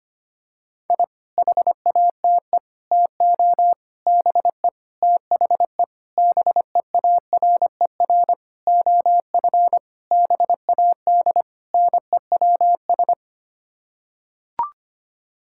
Morse code, heard as a keyed tone, about 25 wpm, 700 Hz. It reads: I HATE TO BE THE BEARER OF BAD NEWS
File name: Sets of 9 Words - 25wpm